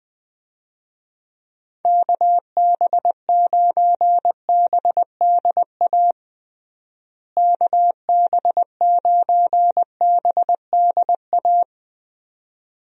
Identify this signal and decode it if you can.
Morse code, heard as a keyed tone, about 21 wpm, 700 Hz. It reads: KB9BDA KB9BDA
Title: USA Call Signs - 6 Characters - Repeated 2x - 20wpm